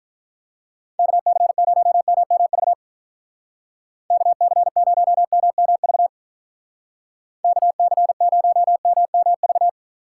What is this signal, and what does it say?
Morse code, heard as a keyed tone, about 45 wpm, 700 Hz. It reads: KC0MMV KC0MMV KC0MMV